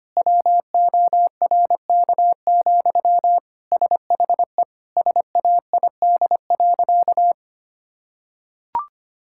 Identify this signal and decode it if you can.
Morse code, heard as a keyed tone, about 25 wpm, 700 Hz. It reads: WORK, SHE SAID.
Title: Sentences from Top 1000 Words - Review Part 6 - 25wpm